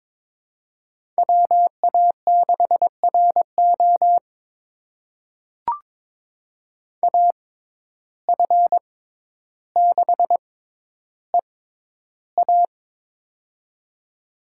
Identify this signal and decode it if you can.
Morse code, heard as a keyed tone, about 23 wpm, spaced as for 11 wpm, 700 Hz. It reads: WA6RO AF6EA
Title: USA Call Signs - 5 Characters - 6x Character Spacing - 22wpm